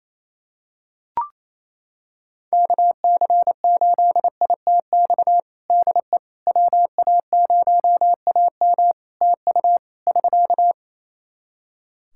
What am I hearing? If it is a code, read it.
Morse code, heard as a keyed tone, about 27 wpm, 700 Hz. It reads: KC8ITX DE WA0AM TU <SK>
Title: Practice QSO - Exchange 4 - Ending - 28wpm